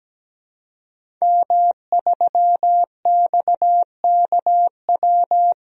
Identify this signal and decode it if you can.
Morse code, heard as a keyed tone, about 17 wpm, 700 Hz. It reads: M3XKW
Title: DXCC-100 Call Signs - 5 Characters - 17wpm